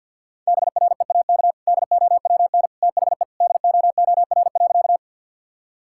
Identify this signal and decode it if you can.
Morse code, heard as a keyed tone, about 50 wpm, 700 Hz. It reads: BREAK DOWN THE DOOR.
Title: Sentences from Top 1000 Words - Review Part 6 - 50wpm